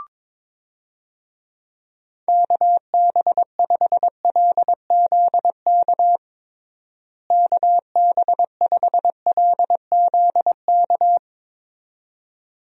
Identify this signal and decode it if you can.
Morse code, heard as a keyed tone, about 22 wpm, 700 Hz. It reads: KB5LZK KB5LZK
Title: USA Call Signs - 6 Characters - Repeated 2x - 22wpm